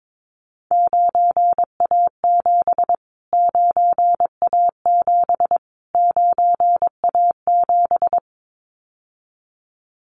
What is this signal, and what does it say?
Morse code, heard as a keyed tone, about 22 wpm, 700 Hz. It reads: 9A7 9A7 9A7
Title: Callsign Prefixes for Top 25 DXCC - Repeated 3x - 22wpm